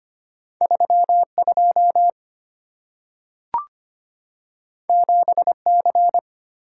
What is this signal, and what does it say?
Morse code, heard as a keyed tone, about 25 wpm, 700 Hz. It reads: 32 7C
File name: Two Letter-Number Combinations - 25wpm